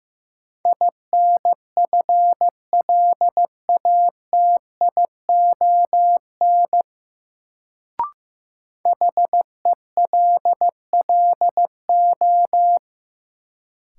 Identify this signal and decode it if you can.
Morse code, heard as a keyed tone, about 16 wpm, 700 Hz. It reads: INFLATION HELLO
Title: Sentences from Top 1100 Words - New Word Drill - 15wpm